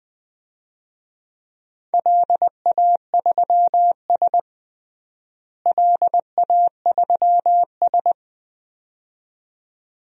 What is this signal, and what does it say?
Morse code, heard as a keyed tone, about 20 wpm, 700 Hz. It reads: LA3S LA3S
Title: DXCC-100 Call Signs - 4 Characters - Repeated 2x - 20wpm